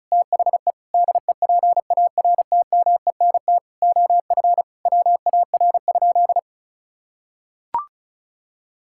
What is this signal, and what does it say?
Morse code, heard as a keyed tone, about 36 wpm, 700 Hz. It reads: THE DEPARTMENT OF WAR?
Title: Sentences from Top 1000 Words - Review Part 5 - 35wpm